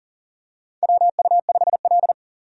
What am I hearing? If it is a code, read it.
Morse code, heard as a keyed tone, about 40 wpm, 700 Hz. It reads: WU5L
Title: USA Call Signs - 4 Characters - 40wpm